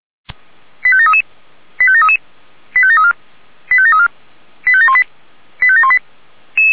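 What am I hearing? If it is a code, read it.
Five-tone selective call (ZVEI), heard as quick runs of five tones, tones of about 70 ms. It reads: 85733 85733 85734 85734 85718 85718
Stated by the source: ZVEI five-tone selective call received by SDR.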